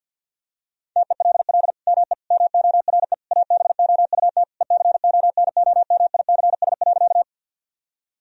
Morse code, heard as a keyed tone, about 48 wpm, 700 Hz. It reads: TELL ME MORE ABOUT ECONOMICS.
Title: Sentences from Top 1000 Words - Review Part 3 - 50wpm